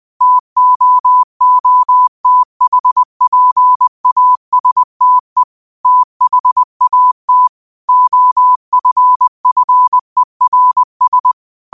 Morse code, heard as a keyed tone, about 20 wpm, 1000 Hz. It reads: TOOTHPASTE THAT OFFERS